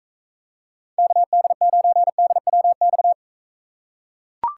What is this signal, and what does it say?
Morse code, heard as a keyed tone, about 41 wpm, 700 Hz. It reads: KD9DWX